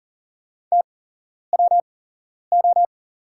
Morse code, heard as a keyed tone, about 41 wpm, spaced as for 11 wpm, 700 Hz. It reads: TWO
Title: 8x Character Spacing - Top 200 Words - 40wpm